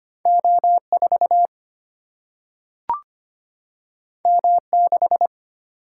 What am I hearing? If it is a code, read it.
Morse code, heard as a keyed tone, about 25 wpm, 700 Hz. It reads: O 4 M 6
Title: Single-Letter Single-Number 25wpm